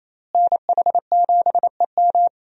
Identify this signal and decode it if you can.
Morse code, heard as a keyed tone, about 28 wpm, 700 Hz. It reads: NH7EM
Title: USA Call Signs - 5 Characters - Repeated 3x - 28wpm